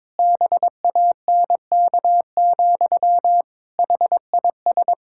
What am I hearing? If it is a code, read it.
Morse code, heard as a keyed tone, about 22 wpm, 700 Hz. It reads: BANK, HIS
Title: Pangram Challenge v2 - 22wpm